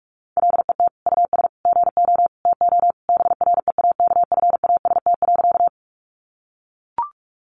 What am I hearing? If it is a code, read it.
Morse code, heard as a keyed tone, about 44 wpm, 700 Hz. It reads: LET US GO TO BREAKFAST.
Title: Sentences from Top 1500 Words 45wpm